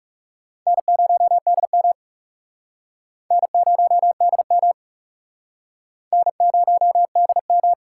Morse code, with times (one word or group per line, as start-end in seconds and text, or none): 0.7-1.9 N0DM
3.3-4.7 N0DM
6.1-7.7 N0DM